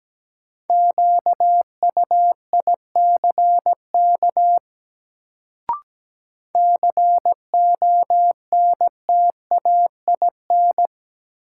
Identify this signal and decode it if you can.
Morse code, heard as a keyed tone, about 17 wpm, 700 Hz. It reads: QUICK CONTAIN
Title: Sentences from Top 500 Words - New Word Drill - 17wpm